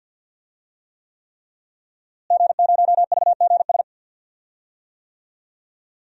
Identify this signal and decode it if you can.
Morse code, heard as a keyed tone, about 50 wpm, 700 Hz. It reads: G0VGS
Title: DXCC-100 Call Signs - 5 Characters - Repeated 3x - 50wpm